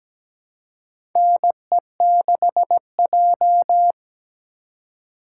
Morse code, heard as a keyed tone, about 17 wpm, 700 Hz. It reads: NE6J
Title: USA Call Signs - 4 Characters - 2x Character Spacing - 17wpm